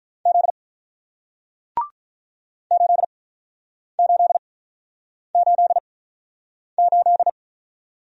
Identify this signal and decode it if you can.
Morse code, heard as a keyed tone, about 48 wpm, spaced as for 9 wpm, 700 Hz. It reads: Z 8888